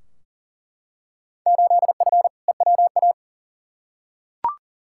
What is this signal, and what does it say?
Morse code, heard as a keyed tone, about 39 wpm, 700 Hz. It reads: 8F EWA
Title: Field Day Exchanges Encore - 40wpm